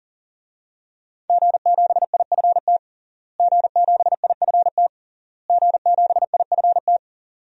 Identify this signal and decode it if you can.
Morse code, heard as a keyed tone, about 37 wpm, 700 Hz. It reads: G7IFT G7IFT G7IFT
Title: DXCC-100 Call Signs - 5 Characters - Repeated 3x - 40wpm